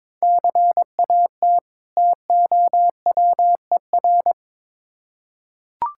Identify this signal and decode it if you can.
Morse code, heard as a keyed tone, about 22 wpm, 700 Hz. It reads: CAT TOWER
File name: Difficult Words with Hints - 22wpm